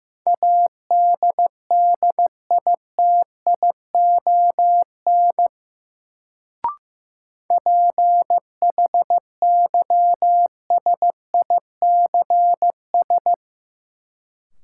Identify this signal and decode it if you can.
Morse code, heard as a keyed tone, about 15 wpm, 700 Hz. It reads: ADDITION PHYSICS
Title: Sentences from Top 1500 Words - New Word Drill Alt - 15wpm